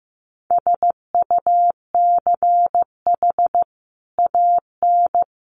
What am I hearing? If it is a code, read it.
Morse code, heard as a keyed tone, about 15 wpm, 700 Hz. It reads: SUCH AN
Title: Sentences from Top 1300 Words 15wpm